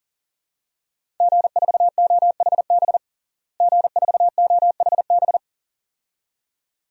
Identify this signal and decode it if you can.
Morse code, heard as a keyed tone, about 40 wpm, 700 Hz. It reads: G4OHB G4OHB
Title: DXCC-100 Call Signs - 5 Characters - Repeated 2x - 40wpm